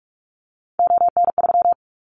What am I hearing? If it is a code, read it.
Morse code, heard as a keyed tone, about 44 wpm, 700 Hz. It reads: ON3
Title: Callsign Prefixes for Top 25 DXCC - Repeated 3x - 45wpm